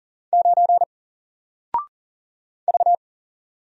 Morse code, heard as a keyed tone, about 40 wpm, 700 Hz. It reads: 9 V
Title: Single Character (Letter:Number:Punctuation:Prosign) Alt 40wpm